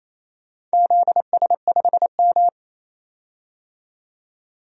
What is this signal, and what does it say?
Morse code, heard as a keyed tone, about 28 wpm, 700 Hz. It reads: ZS5M
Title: DXCC-100 Call Signs - 4 Characters - 28wpm